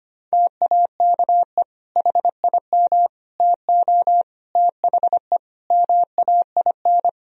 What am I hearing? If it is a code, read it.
Morse code, heard as a keyed tone, about 25 wpm, 700 Hz. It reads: TAKE HIM TO THE MAIN
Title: Sentences from Top 1000 Words - Review Part 4 - 25wpm